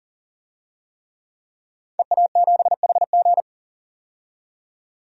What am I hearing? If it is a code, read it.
Morse code, heard as a keyed tone, about 40 wpm, 700 Hz. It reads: EA7HG
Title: DXCC-100 Call Signs - 5 Characters - 40wpm